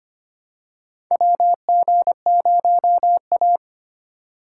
Morse code, heard as a keyed tone, about 25 wpm, 700 Hz. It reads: WG0A
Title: USA Call Signs - 4 Characters - 2x Character Spacing - 25wpm